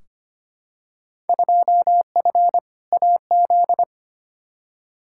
Morse code, heard as a keyed tone, about 25 wpm, 700 Hz. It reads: 2F AZ
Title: Field Day Exchanges - 25wpm